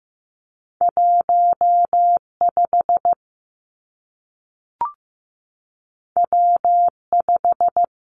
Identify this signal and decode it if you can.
Morse code, heard as a keyed tone, about 15 wpm, 700 Hz. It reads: 1 5 W 5
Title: Two Challenging Letter-Number Combinations 15wpm